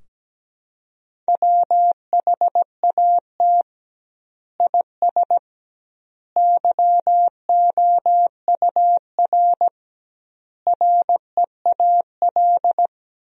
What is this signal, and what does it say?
Morse code, heard as a keyed tone, about 17 wpm, 700 Hz. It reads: WHAT IS YOUR REAL
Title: 2x Word-Spacing - Sentences from Top 300 Words - 17wpm